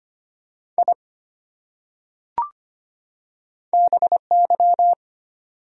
Morse code, heard as a keyed tone, about 25 wpm, 700 Hz. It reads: I BY
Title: Top 100 Words 25wpm